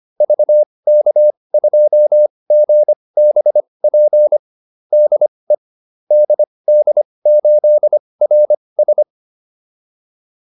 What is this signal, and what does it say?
Morse code, heard as a keyed tone, about 25 wpm, spaced as for 20 wpm, 600 Hz. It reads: VK2GBP DE DD8RS